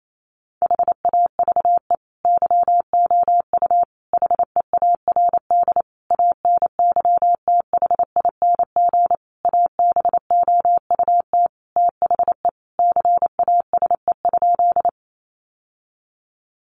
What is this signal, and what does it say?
Morse code, heard as a keyed tone, about 28 wpm, 700 Hz. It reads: HAVE YOU HEARD ANYTHING ABOUT THE CASE?
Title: Sentences from Top 1000 Words 28wpm